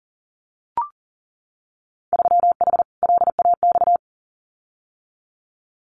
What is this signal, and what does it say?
Morse code, heard as a keyed tone, about 41 wpm, 700 Hz. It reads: 3H LAX
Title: Winter Field Day Exchanges 40wpm